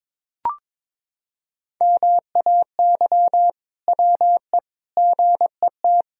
Morse code, heard as a keyed tone, about 22 wpm, 700 Hz. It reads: MAY WE GET
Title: Sentences from Top 1000 Words - Review Part 4 - 22wpm